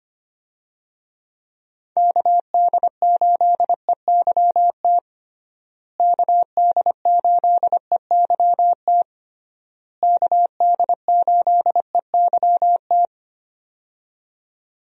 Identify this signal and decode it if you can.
Morse code, heard as a keyed tone, about 25 wpm, 700 Hz. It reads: KD8EYT KD8EYT KD8EYT
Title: USA Call Signs - 6 Characters - Repeated 3x - 25wpm